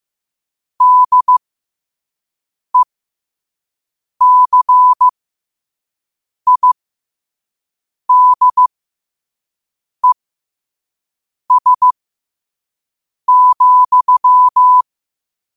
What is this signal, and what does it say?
Morse code, heard as a keyed tone, about 15 wpm, spaced as for 5 wpm, 1000 Hz. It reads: DECIDES,